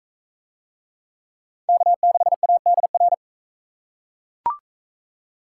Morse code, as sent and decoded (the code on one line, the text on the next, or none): -.- -.... .- -.. .-.
K6ADR